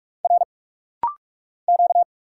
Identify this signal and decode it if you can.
Morse code, heard as a keyed tone, about 46 wpm, 700 Hz. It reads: R Q
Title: Single Letters - Rapid-Fire 45wpm